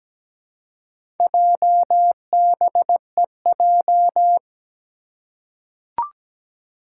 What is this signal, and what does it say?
Morse code, heard as a keyed tone, about 17 wpm, 700 Hz. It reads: JBEJ
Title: Four Challenging Letter-Number Combinations 17wpm